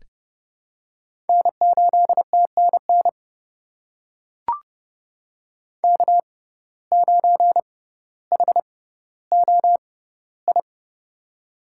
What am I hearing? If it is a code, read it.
Morse code, heard as a keyed tone, about 30 wpm, spaced as for 15 wpm, 700 Hz. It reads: N8TNN K9HOI
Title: USA Call Signs - 5 Characters - 6x Character Spacing - 30wpm